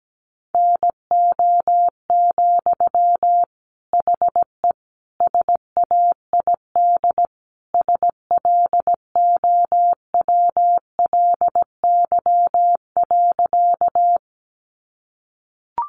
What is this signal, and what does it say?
Morse code, heard as a keyed tone, about 17 wpm, 700 Hz. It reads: NO, HE SAID SLOWLY.
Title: Sentences from Top 1100 Words 17wpm